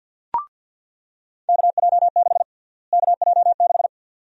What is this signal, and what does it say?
Morse code, heard as a keyed tone, about 46 wpm, 700 Hz. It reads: KJ6 KJ6
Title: USA Call Sign Prefixes - Repeated 2x - 50wpm